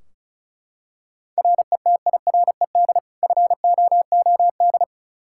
Morse code, heard as a keyed tone, about 36 wpm, 700 Hz. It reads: RETIRED FOOD
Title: K1USN SST QSO Practice 35wpm